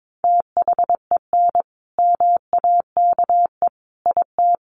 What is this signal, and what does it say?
Morse code, heard as a keyed tone, about 22 wpm, 700 Hz. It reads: THEN MAKE IT
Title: Sentences from Top 100 Words 22wpm